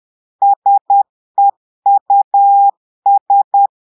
Morse code, heard as a keyed tone, about 10 wpm, 800 Hz. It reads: SEUS
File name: portugais texte en clair 2 vitesse 10 mots minutes